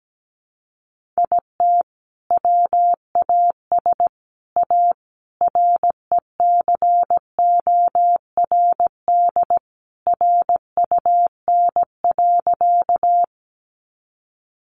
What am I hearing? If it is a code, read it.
Morse code, heard as a keyed tone, about 17 wpm, 700 Hz. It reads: IT WAS A RECORD RUN.
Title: Sentences from Top 500 Words 17wpm